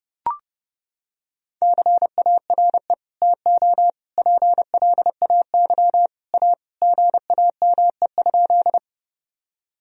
Morse code, heard as a keyed tone, about 32 wpm, 700 Hz. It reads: CARE TO PLAY A GAME?
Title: Sentences from Top 1000 Words - Review Part 1 - 30wpm